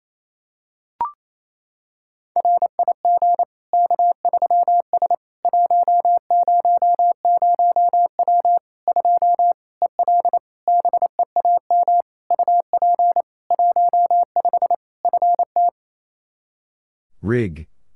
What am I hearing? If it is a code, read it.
Morse code, heard as a keyed tone, about 28 wpm, 700 Hz. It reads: RIG K3S 100W 2 EL BEAM UP 15 FT
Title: Practice QSO - Exchange 2 - Info - 28wpm